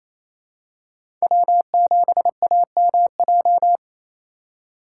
Morse code, heard as a keyed tone, about 28 wpm, 700 Hz. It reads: W7AMJ
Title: USA Call Signs - 5 Characters - 4x Character Spacing - 28wpm